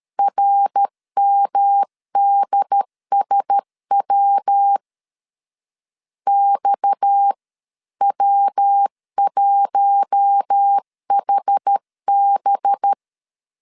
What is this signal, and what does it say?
Morse code, heard as a keyed tone, about 13 wpm, 800 Hz. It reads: RMDSW XW1HB
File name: CW / DET / 1 6 1500C S C 12wpm 3533